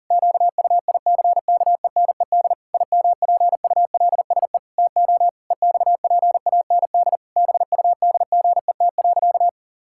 Morse code, heard as a keyed tone, about 40 wpm, 700 Hz. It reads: QUICKENED IMPULSE TO EXPAND BUDGET.
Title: Pangram Challenge v2 - 40wpm